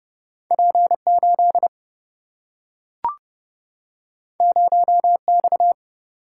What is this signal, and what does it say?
Morse code, heard as a keyed tone, about 30 wpm, 700 Hz. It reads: P8 0X
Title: Single-Letter Single-Number 30wpm